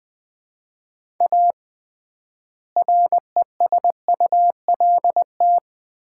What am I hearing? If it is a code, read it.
Morse code, heard as a keyed tone, about 20 wpm, 700 Hz. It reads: A RESULT
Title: 3x Word-Spacing - Sets of 3 Words - 20wpm